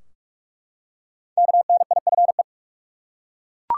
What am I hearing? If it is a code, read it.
Morse code, heard as a keyed tone, about 41 wpm, 700 Hz. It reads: KNIFE
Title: Five Letter Words - 45wpm